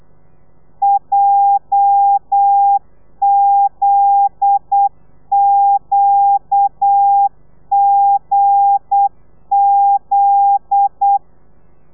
Morse code, heard as a keyed tone, about 8 wpm, 800 Hz. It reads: JZQGZ
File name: telegrafia / i2tfj / Lez9 35